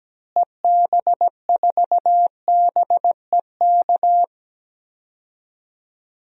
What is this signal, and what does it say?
Morse code, heard as a keyed tone, about 17 wpm, 700 Hz. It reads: EB4BEK
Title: DXCC-100 Call Signs - 6 Characters - Repeated 3x - 17wpm